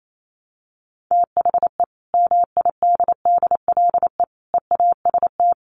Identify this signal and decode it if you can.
Morse code, heard as a keyed tone, about 28 wpm, 700 Hz. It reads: THE MIDDLE EAST
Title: Sets of 3 Words 28wpm